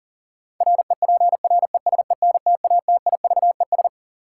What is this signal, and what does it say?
Morse code, heard as a keyed tone, about 40 wpm, 700 Hz. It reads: REPRESENTATIVES
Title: Sets of 5 Words Encore - New Word Drill - 40wpm